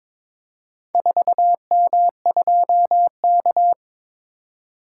Morse code, heard as a keyed tone, about 22 wpm, 700 Hz. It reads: 4M2K
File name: practice / 22wpm / Four Letter-Number Combinations - 22wpm